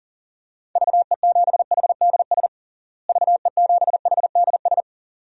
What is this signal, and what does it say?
Morse code, heard as a keyed tone, about 37 wpm, 700 Hz. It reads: VE7HDS VE7HDS
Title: DXCC-100 Call Signs - 6 Characters - Repeated 2x - 40wpm